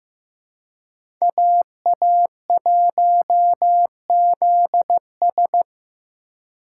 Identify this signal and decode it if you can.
Morse code, heard as a keyed tone, about 18 wpm, 700 Hz. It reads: AA1ZS
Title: Speed Racing - USA 2x2 Call Signs 15wpm